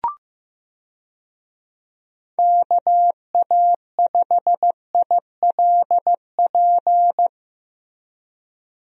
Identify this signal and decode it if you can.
Morse code, heard as a keyed tone, about 15 wpm, 700 Hz. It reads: KA5ILP
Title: USA Call Signs - 6 Characters - Repeated 2x - 15wpm